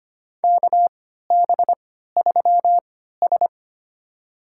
Morse code, heard as a keyed tone, about 25 wpm, spaced as for 14 wpm, 700 Hz. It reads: KB3S
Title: USA Call Signs - 3x Character Spacing - 25wpm